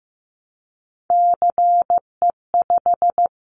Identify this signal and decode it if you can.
Morse code, heard as a keyed tone, about 15 wpm, 700 Hz. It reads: CE5
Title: Callsign Prefixes for Top 100 DXCC - 15wpm